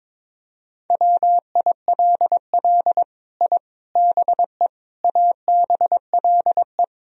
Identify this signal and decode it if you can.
Morse code, heard as a keyed tone, about 22 wpm, 700 Hz. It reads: WILL I BE ABLE
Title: Sentences from Top 1000 Words - Review Part 3 - 22wpm